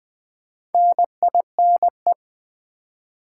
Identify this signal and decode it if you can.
Morse code, heard as a keyed tone, about 20 wpm, 690 Hz. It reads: NINE
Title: Build A Word - 2 Letters At A Time - 4-Letter Words - 20wpm